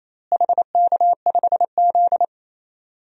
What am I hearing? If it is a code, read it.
Morse code, heard as a keyed tone, about 28 wpm, 700 Hz. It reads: HK5Z
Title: DXCC-100 Call Signs - 4 Characters - Repeated 2x - 28wpm